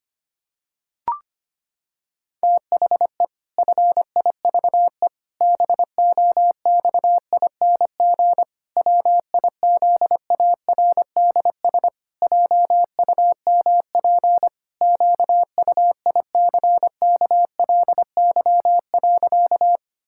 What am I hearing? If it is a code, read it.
Morse code, heard as a keyed tone, about 25 wpm, 700 Hz. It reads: THE FIVE BOXING WIZARDS JUMP QUICKLY.
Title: Pangram Challenge v2 - 25wpm